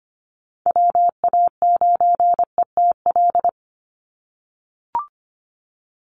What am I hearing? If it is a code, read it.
Morse code, heard as a keyed tone, about 25 wpm, 700 Hz. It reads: WA9ETL
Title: USA Call Signs 25wpm